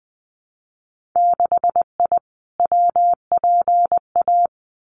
Morse code, heard as a keyed tone, about 20 wpm, 700 Hz. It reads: 6I WPA
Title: Winter Field Day Exchanges Encore 20wpm